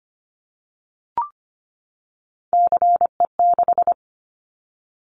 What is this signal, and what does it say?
Morse code, heard as a keyed tone, about 25 wpm, 700 Hz. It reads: CE6
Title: Callsign Prefixes for Top 100 DXCC - 25wpm